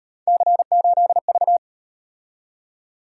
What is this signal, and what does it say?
Morse code, heard as a keyed tone, about 38 wpm, 700 Hz. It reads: C8V